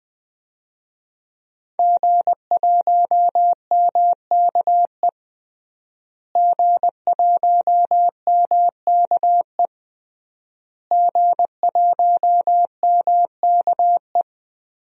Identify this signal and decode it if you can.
Morse code, heard as a keyed tone, about 21 wpm, 700 Hz. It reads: G1MKE G1MKE G1MKE
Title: DXCC-100 Call Signs - 5 Characters - Repeated 3x - 20wpm